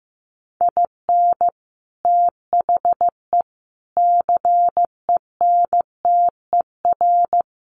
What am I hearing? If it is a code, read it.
Morse code, heard as a keyed tone, about 15 wpm, 700 Hz. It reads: IN THE CENTER
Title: Sets of 4 Words 15wpm